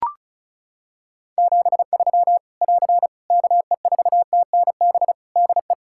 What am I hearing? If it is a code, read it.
Morse code, heard as a keyed tone, about 36 wpm, 700 Hz. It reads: 73 <AR> KE4TNB DE
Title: K1USN SST QSO Practice 35wpm